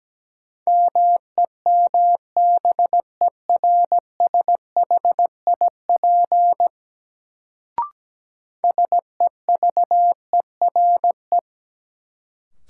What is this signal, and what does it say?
Morse code, heard as a keyed tone, about 17 wpm, 700 Hz. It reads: MEMBERSHIP SEVERE
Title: Sentences from Top 1400 Words - New Word Drill - 17wpm